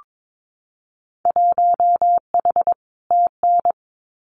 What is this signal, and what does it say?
Morse code, heard as a keyed tone, about 22 wpm, 700 Hz. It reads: 1H TN
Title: Winter Field Day Exchanges 22wpm